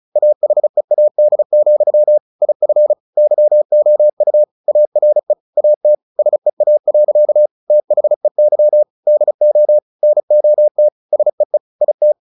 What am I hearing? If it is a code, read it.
Morse code, heard as a keyed tone, about 34 wpm, 600 Hz. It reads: AHEAD, IF YOU ARE AT SEA. THEY DO NOT SEE IT
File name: the old man and the sea / the old man and the sea 1 35wpm